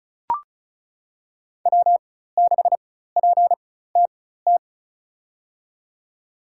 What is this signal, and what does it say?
Morse code, heard as a keyed tone, about 36 wpm, spaced as for 16 wpm, 700 Hz. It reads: W6PTT
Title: USA Call Signs - 5 Characters - 4x Character Spacing - 35wpm